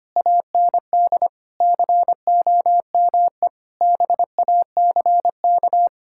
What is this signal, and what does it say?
Morse code, heard as a keyed tone, about 25 wpm, 700 Hz. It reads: AND COME BACK
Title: Sentences from Top 1000 Words - Review Part 6 - 25wpm